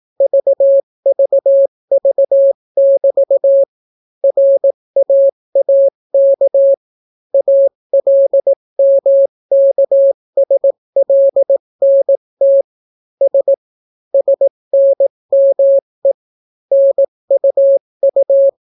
Morse code, heard as a keyed tone, about 18 wpm, 550 Hz. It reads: VVV<BT> RAAK ALMKSLNT S SNME NUU